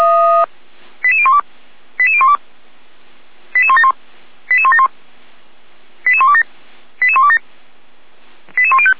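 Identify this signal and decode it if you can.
Five-tone selective call (ZVEI), heard as quick runs of five tones, tones of about 70 ms. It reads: 88012 88012 88171 88171 88127 88127 88105
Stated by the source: ZVEI five-tone call received by SDR, archived by Feuerwehr Orlamünde.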